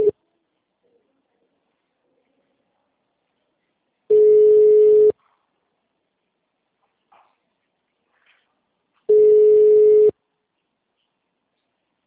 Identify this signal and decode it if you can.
Ringback tone (Europe, 425 Hz).